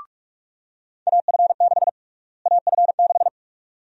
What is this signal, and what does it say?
Morse code, heard as a keyed tone, about 48 wpm, 700 Hz. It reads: AF6 AF6
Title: USA Call Sign Prefixes - Repeated 2x - 45wpm